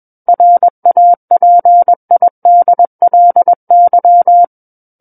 Morse code, heard as a keyed tone, about 21 wpm, 700 Hz. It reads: RAPIDLY
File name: Book One / 12-Chapter 12 - 21 WPM